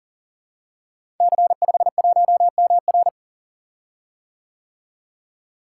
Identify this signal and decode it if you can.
Morse code, heard as a keyed tone, about 41 wpm, 700 Hz. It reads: C51MR
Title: Code Groups - 5 Letter:Numbers - 40wpm